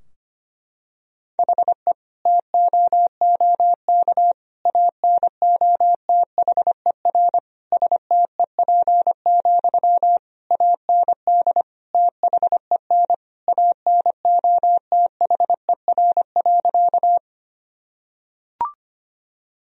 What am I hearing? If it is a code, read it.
Morse code, heard as a keyed tone, about 25 wpm, 700 Hz. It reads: HE TOOK ANOTHER STEP, AND THEN ANOTHER.
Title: Sentences from Top 1000 Words - Review Part 2 - 25wpm